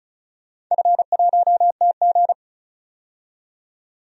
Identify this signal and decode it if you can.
Morse code, heard as a keyed tone, about 36 wpm, 700 Hz. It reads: F1TG
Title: DXCC-100 Call Signs - 4 Characters - 35wpm